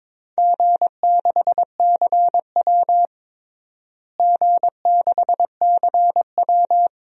Morse code, heard as a keyed tone, about 22 wpm, 700 Hz. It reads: G6CW G6CW
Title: DXCC-100 Call Signs - 4 Characters - Repeated 3x - 22wpm